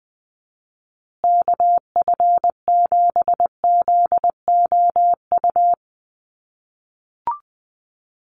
Morse code, heard as a keyed tone, about 20 wpm, 700 Hz. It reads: KF7ZOU
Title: USA Call Signs Repeated 3x 20wpm